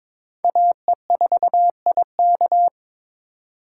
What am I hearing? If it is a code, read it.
Morse code, heard as a keyed tone, about 22 wpm, 700 Hz. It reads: AE4IK
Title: USA Call Signs - 5 Characters - 4x Character Spacing - 22wpm